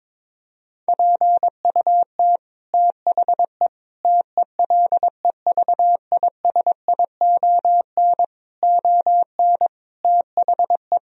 Morse code, heard as a keyed tone, about 22 wpm, 700 Hz. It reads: PUT THE TELEVISION ON THE
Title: Sentences from Top 1000 Words - Review Part 4 - 22wpm